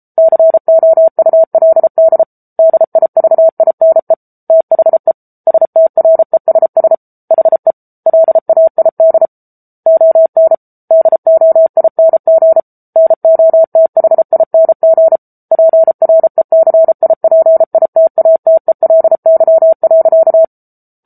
Morse code, heard as a keyed tone, about 34 wpm, 650 Hz. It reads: COULD DIVINE THE STRESS HE LAID ON DOING NOTHING PRECIPITATELY.
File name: Book Two / 07-Chapter 7 - 33 WPM